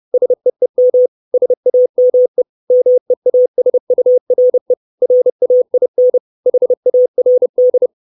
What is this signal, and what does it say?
Morse code, heard as a keyed tone, about 30 wpm, 500 Hz. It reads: SEEM SAME MEASURE RAIN HARD
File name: 632 Words 30WPM